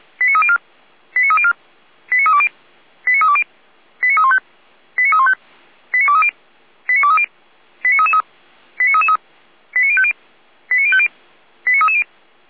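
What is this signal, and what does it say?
Five-tone selective call (ZVEI), heard as quick runs of five tones, tones of about 70 ms. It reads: 89304 89304 89320 89320 89315 89315 89230 89230 89333 89333 89050 89050 89330
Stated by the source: ZVEI five-tone selective call received on an SDR.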